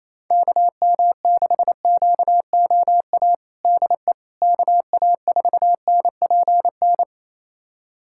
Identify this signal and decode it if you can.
Morse code, heard as a keyed tone, about 28 wpm, 700 Hz. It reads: KM6QOA DE KA4NPN
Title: USA Callsign DE Callsign - 6 Characters - 28wpm